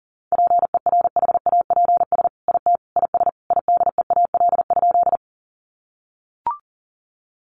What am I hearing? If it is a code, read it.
Morse code, heard as a keyed tone, about 40 wpm, 700 Hz. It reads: PERHAPS IT IS IDEAL?
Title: Sentences from Top 1400 Words 40wpm